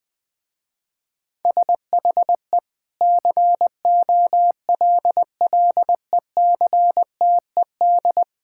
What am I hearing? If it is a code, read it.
Morse code, heard as a keyed tone, about 20 wpm, 700 Hz. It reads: SHE COLLECTED
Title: Sentences with 2-Word Phrases - Spoken Beginning - 20wpm